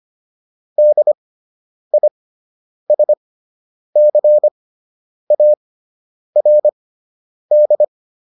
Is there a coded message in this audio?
Yes: Morse code from 0.8 to 7.8 s, heard as a keyed tone, about 25 wpm, spaced as for 9 wpm, 600 Hz. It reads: DISCARD